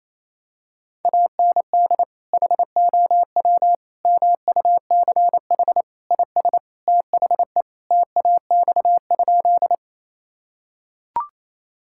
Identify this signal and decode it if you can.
Morse code, heard as a keyed tone, about 28 wpm, 700 Hz. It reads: AND HOW MUCH IS THE TAX?
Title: Sentences from Top 1000 Words - Review Part 5 - 28wpm